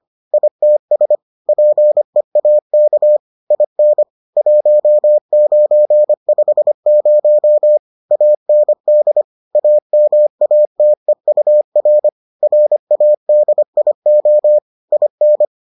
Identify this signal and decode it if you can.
Morse code, heard as a keyed tone, about 25 wpm, 600 Hz. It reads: ITS PEAK IN 1950 AND AMATEUR RADIO IN